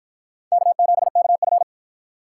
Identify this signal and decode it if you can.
Morse code, heard as a keyed tone, about 53 wpm, 700 Hz. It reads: K7KF